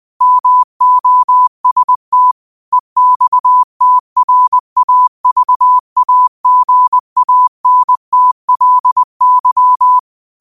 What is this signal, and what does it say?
Morse code, heard as a keyed tone, about 20 wpm, 1000 Hz. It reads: MOST EXTRAVAGANTLY
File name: 20wpm